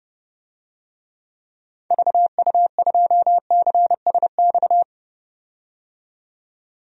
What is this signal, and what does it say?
Morse code, heard as a keyed tone, about 30 wpm, 700 Hz. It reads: VU2CSX
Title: DXCC-100 Call Signs - 6 Characters - 30wpm